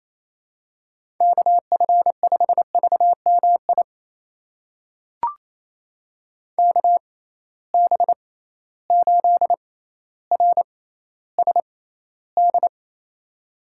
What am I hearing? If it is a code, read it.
Morse code, heard as a keyed tone, about 28 wpm, spaced as for 14 wpm, 700 Hz. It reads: KF5VMI KB8RSD
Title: USA Call Signs - 6 Characters - 6x Character Spacing - 28wpm